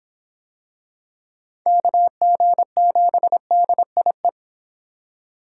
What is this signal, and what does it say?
Morse code, heard as a keyed tone, about 25 wpm, 700 Hz. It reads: KG7DIE